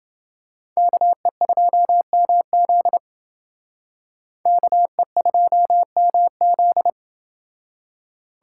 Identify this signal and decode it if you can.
Morse code, heard as a keyed tone, about 29 wpm, 700 Hz. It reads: KE2MZ KE2MZ